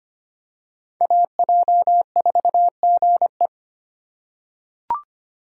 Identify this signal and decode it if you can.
Morse code, heard as a keyed tone, about 25 wpm, 700 Hz. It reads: AJ4GE
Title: USA Call Signs - 2x Character Spacing - 25wpm